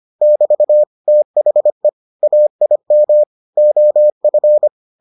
Morse code, heard as a keyed tone, about 25 wpm, 600 Hz. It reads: <BT> THE AIM OF